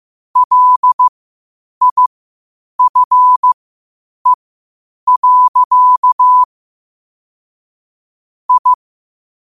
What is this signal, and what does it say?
Morse code, heard as a keyed tone, about 15 wpm, spaced as for 8 wpm, 1000 Hz. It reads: LIFE. I
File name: qotdfiles / 8wpm